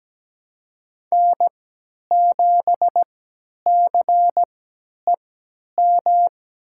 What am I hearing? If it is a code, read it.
Morse code, heard as a keyed tone, about 17 wpm, spaced as for 10 wpm, 700 Hz. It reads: N7CEM
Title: USA Call Signs - 5 Characters - 3x Character Spacing - 17wpm